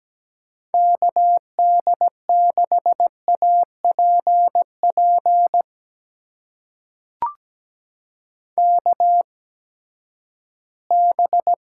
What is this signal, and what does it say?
Morse code, heard as a keyed tone, about 17 wpm, spaced as for 12 wpm, 700 Hz. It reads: KD6APP KB
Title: USA Call Signs - 6 Characters - 8x Character Spacing - 17wpm